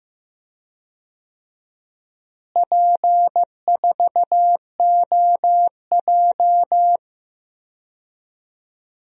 Morse code, heard as a keyed tone, about 15 wpm, 700 Hz. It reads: P4OJ
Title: DXCC-100 Call Signs - 4 Characters - Repeated 2x - 15wpm